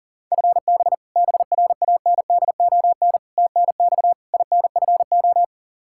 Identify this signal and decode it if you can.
Morse code, heard as a keyed tone, about 40 wpm, 700 Hz. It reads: FB BRANDON TNX INFO
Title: Practice QSO - Exchange 3 - More Info - 40wpm